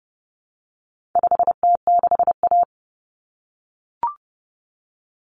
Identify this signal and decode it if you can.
Morse code, heard as a keyed tone, about 30 wpm, 700 Hz. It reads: 5T6A
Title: Common Cut Numbers - 4-Digit 30wpm